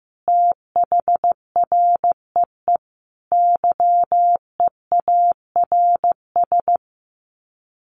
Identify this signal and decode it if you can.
Morse code, heard as a keyed tone, about 15 wpm, 700 Hz. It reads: THREE YEARS
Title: Sets of 5 Words Encore2 15wpm